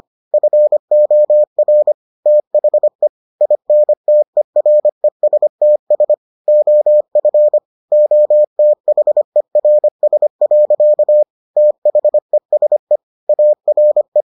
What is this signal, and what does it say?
Morse code, heard as a keyed tone, about 25 wpm, 600 Hz. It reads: FOR THE INTERESTS OF OTHERS. THESE ARE